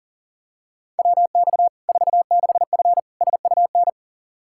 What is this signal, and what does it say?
Morse code, heard as a keyed tone, about 40 wpm, 700 Hz. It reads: WX 46F SUN
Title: Practice QSO - Exchange 2 - Info - 40wpm